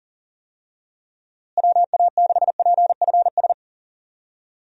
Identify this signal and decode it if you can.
Morse code, heard as a keyed tone, about 40 wpm, 700 Hz. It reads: WA6PFS
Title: USA Call Signs - 6 Characters - Repeated 2x - 40wpm